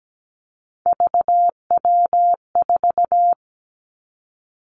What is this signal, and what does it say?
Morse code, heard as a keyed tone, about 17 wpm, 700 Hz. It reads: VW4
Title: Three Challenging Letter-Number Combinations 17wpm